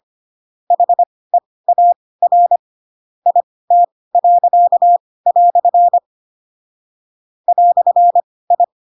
Morse code, heard as a keyed tone, about 25 wpm, spaced as for 18 wpm, 700 Hz. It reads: HEAR IT." "I
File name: Pinocchio / Pinocchio ch2 12 25 ik1hge